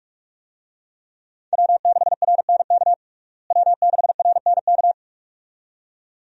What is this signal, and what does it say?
Morse code, heard as a keyed tone, about 45 wpm, 700 Hz. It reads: W6RNK W6RNK
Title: USA Call Signs - 5 Characters - Repeated 2x - 45wpm